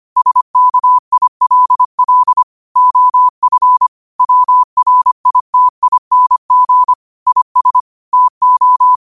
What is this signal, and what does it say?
Morse code, heard as a keyed tone, about 25 wpm, 1000 Hz. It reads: SKILL OF WRITING IS TO